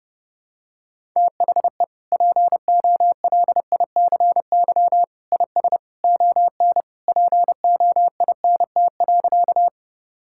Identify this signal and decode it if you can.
Morse code, heard as a keyed tone, about 30 wpm, 700 Hz. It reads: THE POLICY IS ON POINT.
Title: Sentences from Top 1000 Words - Review Part 5 - 30wpm